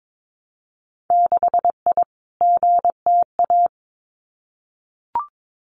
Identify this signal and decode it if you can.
Morse code, heard as a keyed tone, about 22 wpm, 700 Hz. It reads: 6I GTA
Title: Winter Field Day Exchanges Encore 22wpm